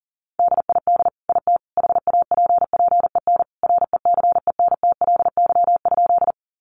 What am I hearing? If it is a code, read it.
Morse code, heard as a keyed tone, about 40 wpm, 700 Hz. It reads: DID IT HAPPEN RECENTLY?
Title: Sentences from Top 1200 Words 40wpm